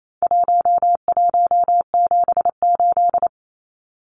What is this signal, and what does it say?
Morse code, heard as a keyed tone, about 28 wpm, 700 Hz. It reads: 1178
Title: Numbers - 4-Digit 28wpm